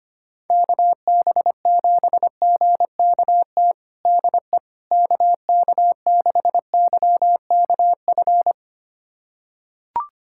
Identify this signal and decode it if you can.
Morse code, heard as a keyed tone, about 25 wpm, 700 Hz. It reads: KB7GKT DE KK6YKF
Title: USA Callsign DE Callsign 25wpm